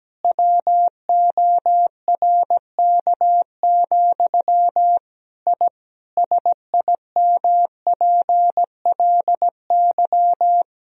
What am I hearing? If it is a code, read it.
Morse code, heard as a keyed tone, about 17 wpm, 700 Hz. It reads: WORK, I SIMPLY
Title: Sentences from Top 1000 Words - Review Part 4 - 17wpm